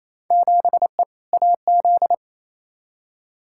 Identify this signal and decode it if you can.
Morse code, heard as a keyed tone, about 28 wpm, 700 Hz. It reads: 7E AZ
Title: Field Day Exchanges Encore - 28wpm